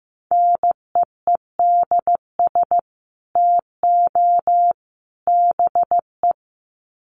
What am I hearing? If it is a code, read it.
Morse code, heard as a keyed tone, about 15 wpm, 700 Hz. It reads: NEEDS TO BE
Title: Sets of 3 Words 15wpm